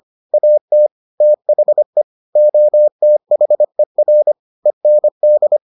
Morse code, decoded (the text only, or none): AT THE OTHER END